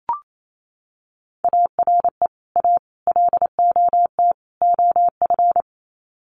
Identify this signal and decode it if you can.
Morse code, heard as a keyed tone, about 28 wpm, 700 Hz. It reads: ARE A LOT OF
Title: Sets of 4 Words 28wpm